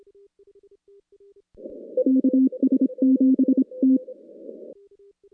Morse code, heard as a keyed tone, about 26 wpm, 260 Hz. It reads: KS7T
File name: contests / CW-SS-k5tr- -2100